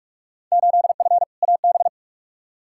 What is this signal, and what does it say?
Morse code, heard as a keyed tone, about 41 wpm, 700 Hz. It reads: 8F AB
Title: Field Day Exchanges Encore - 45wpm